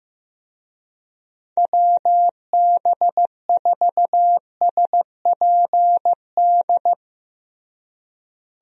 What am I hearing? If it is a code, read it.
Morse code, heard as a keyed tone, about 15 wpm, 700 Hz. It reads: WB4SPD
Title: USA Call Signs - 6 Characters - Repeated 2x - 15wpm